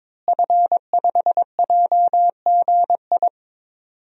Morse code, heard as a keyed tone, about 22 wpm, 700 Hz. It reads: F5JGI
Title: DXCC-100 Call Signs - 5 Characters - Repeated 2x - 22wpm